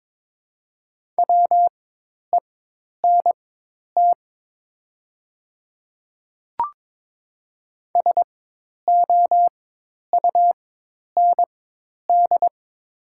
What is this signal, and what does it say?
Morse code, heard as a keyed tone, about 22 wpm, spaced as for 10 wpm, 700 Hz. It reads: WENT SOUND
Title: 4x Character Spacing - Top 300 Words - 22wpm